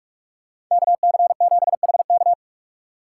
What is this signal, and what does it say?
Morse code, heard as a keyed tone, about 43 wpm, 700 Hz. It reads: KC7HK